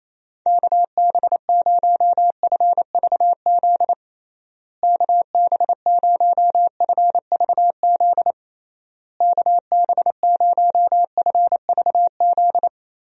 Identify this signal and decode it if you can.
Morse code, heard as a keyed tone, about 28 wpm, 700 Hz. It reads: KB0FVZ KB0FVZ KB0FVZ
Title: USA Call Signs - 6 Characters - Repeated 3x - 28wpm